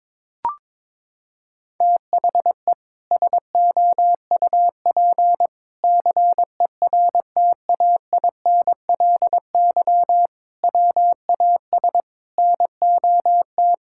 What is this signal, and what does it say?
Morse code, heard as a keyed tone, about 22 wpm, 700 Hz. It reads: THE SOUP CERTAINLY WAS NOT
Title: Sentences from Top 1000 Words - Review Part 6 - 22wpm